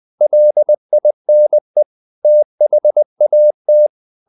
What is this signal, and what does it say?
Morse code, heard as a keyed tone, about 20 wpm, 600 Hz. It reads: LINE THAT
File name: the old man and the sea part 2 20wpm